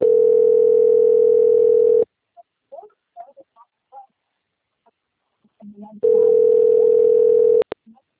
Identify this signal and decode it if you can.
Ringback tone (North America), 440 + 480 Hz.